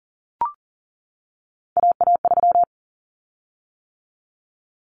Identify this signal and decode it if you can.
Morse code, heard as a keyed tone, about 40 wpm, 700 Hz. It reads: AA3
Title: Common Cut Numbers - 3-Digit 40wpm